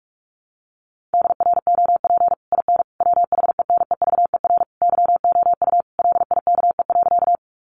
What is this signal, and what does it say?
Morse code, heard as a keyed tone, about 44 wpm, 700 Hz. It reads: DROP IN WHENEVER YOU LIKE.
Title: Sentences from Top 1400 Words 45wpm